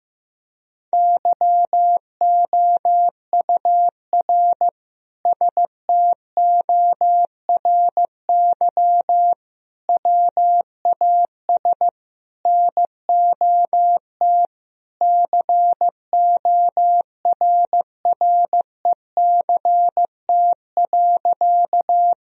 Morse code, heard as a keyed tone, about 15 wpm, 700 Hz. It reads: YOUR STORY WAS NOT CORRECT.
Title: Sentences from Top 1000 Words - Review Part 2 - 15wpm